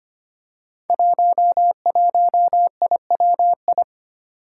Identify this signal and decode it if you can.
Morse code, heard as a keyed tone, about 25 wpm, 700 Hz. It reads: 11IWI
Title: Five Challenging Letter-Number Combinations 25wpm